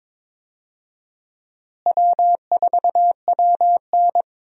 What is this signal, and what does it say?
Morse code, heard as a keyed tone, about 22 wpm, 700 Hz. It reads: W4WN
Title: USA Call Signs - 4 Characters - Repeated 2x - 22wpm